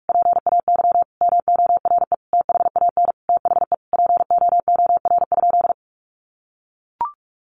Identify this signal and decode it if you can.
Morse code, heard as a keyed tone, about 44 wpm, 700 Hz. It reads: PAY MORE THAN THE POOR?
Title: Sentences from Top 1100 Words 45wpm